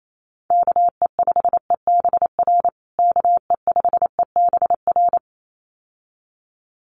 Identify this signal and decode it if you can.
Morse code, heard as a keyed tone, about 28 wpm, 700 Hz. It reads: KE5EBR KE5EBR
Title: USA Call Signs Repeated 2x 28wpm